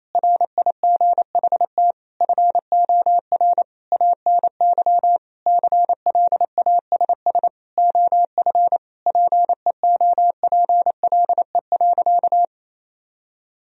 Morse code, heard as a keyed tone, about 28 wpm, 700 Hz. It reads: RIGHT FOR ANY CLASS OF PEOPLE.
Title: Sentences from Top 1000 Words - Review Part 2 - 28wpm